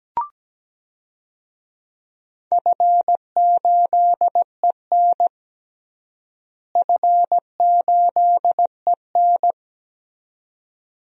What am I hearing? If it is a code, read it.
Morse code, heard as a keyed tone, about 17 wpm, 700 Hz. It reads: F8EN F8EN
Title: DXCC-100 Call Signs - 4 Characters - Repeated 2x - 17wpm